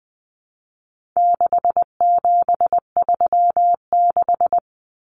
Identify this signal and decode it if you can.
Morse code, heard as a keyed tone, about 20 wpm, 700 Hz. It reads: 6736
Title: Common Cut Numbers - 4-Digit 20wpm